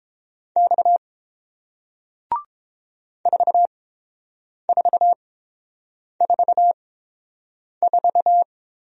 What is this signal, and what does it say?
Morse code, heard as a keyed tone, about 30 wpm, spaced as for 7 wpm, 700 Hz. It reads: X 4444